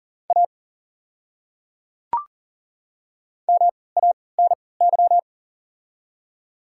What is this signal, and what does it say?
Morse code, heard as a keyed tone, about 40 wpm, spaced as for 23 wpm, 690 Hz. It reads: A MANY
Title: 3x Character Spacing - Top 100 Words - 40wpm